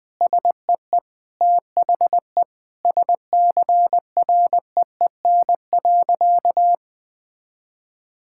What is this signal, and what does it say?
Morse code, heard as a keyed tone, about 20 wpm, 700 Hz. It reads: SEE THE SCREEN.
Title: Sentences from Top 1000 Words - Review Part 6 - 20wpm